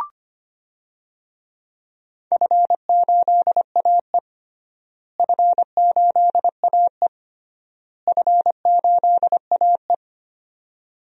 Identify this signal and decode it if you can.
Morse code, heard as a keyed tone, about 25 wpm, 700 Hz. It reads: F8AE F8AE F8AE
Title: DXCC-100 Call Signs - 4 Characters - Repeated 3x - 25wpm